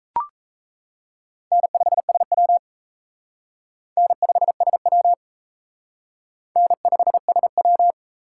Morse code, heard as a keyed tone, about 34 wpm, 700 Hz. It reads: N5SW N5SW N5SW
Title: Speed Racing - USA 1x2 Call Signs 28wpm